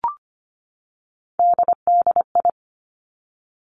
Morse code, heard as a keyed tone, about 25 wpm, 700 Hz. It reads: DDI
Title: Three Challenging Letter-Number Combinations 25wpm